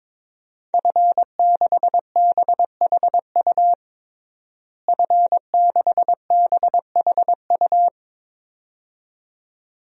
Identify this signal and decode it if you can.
Morse code, heard as a keyed tone, about 22 wpm, 700 Hz. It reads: F6BHU F6BHU
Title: DXCC-100 Call Signs - 5 Characters - Repeated 2x - 22wpm